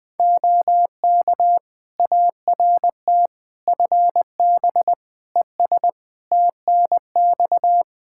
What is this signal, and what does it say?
Morse code, heard as a keyed tone, about 21 wpm, 700 Hz. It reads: OK ART FB ES TNX
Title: Practice QSO - Exchange 2 - Info - 20wpm